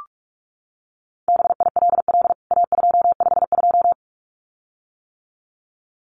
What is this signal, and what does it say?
Morse code, heard as a keyed tone, about 44 wpm, 700 Hz. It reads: BILL A252
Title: CWT Exchange 45wpm